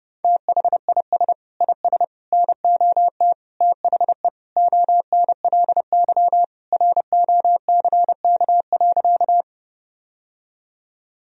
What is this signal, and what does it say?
Morse code, heard as a keyed tone, about 32 wpm, 700 Hz. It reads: THIS IS NOT THE ONLY ROCK.
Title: Sentences from Top 1000 Words - Review Part 2 - 30wpm